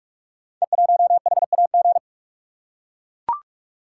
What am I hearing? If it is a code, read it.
Morse code, heard as a keyed tone, about 44 wpm, 690 Hz. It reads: E1HAG
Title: Code Groups - 5 Letter:Numbers - 45wpm